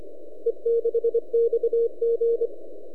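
Morse code, heard as a keyed tone, about 26 wpm, 460 Hz. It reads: E6XG
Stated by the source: E6XG heard on 10 m CW, recorded by @n6ml.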